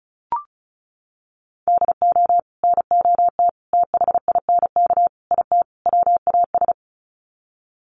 Morse code, heard as a keyed tone, about 34 wpm, 700 Hz. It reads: DO NOT THINK IT WAS
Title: Sets of 5 Words Encore 35wpm